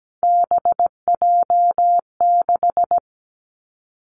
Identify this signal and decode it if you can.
Morse code, heard as a keyed tone, about 17 wpm, 700 Hz. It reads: BJ6
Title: Three Challenging Letter-Number Combinations 17wpm